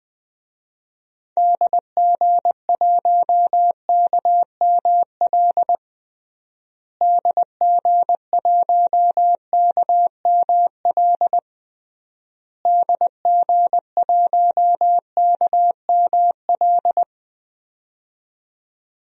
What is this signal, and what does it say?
Morse code, heard as a keyed tone, about 20 wpm, 700 Hz. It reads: DG1KML DG1KML DG1KML
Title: DXCC-100 Call Signs - 6 Characters - Repeated 3x - 20wpm